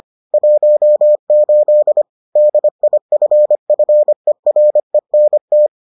Morse code, heard as a keyed tone, about 25 wpm, 600 Hz. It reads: 18 DIFFERENT